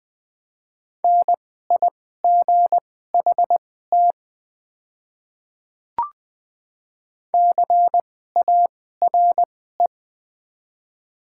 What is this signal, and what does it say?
Morse code, heard as a keyed tone, about 20 wpm, spaced as for 15 wpm, 700 Hz. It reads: NIGHT CARE
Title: 2x Character Spacing - Top 500 Words - 20wpm